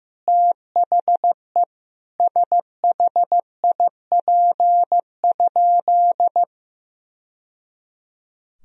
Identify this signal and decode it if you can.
Morse code, heard as a keyed tone, about 15 wpm, 700 Hz. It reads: THE SHIP?
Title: Sentences from Top 1000 Words - Review Part 2 - 15wpm